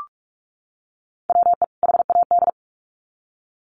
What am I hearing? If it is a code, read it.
Morse code, heard as a keyed tone, about 44 wpm, 700 Hz. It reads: WE HAD
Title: Sets of 2 Words 45wpm